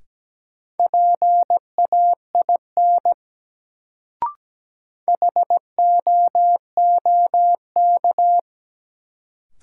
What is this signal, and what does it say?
Morse code, heard as a keyed tone, about 17 wpm, 700 Hz. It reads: PAIN HOOK
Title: Sentences from Top 1600 Words - New Word Drill - 17wpm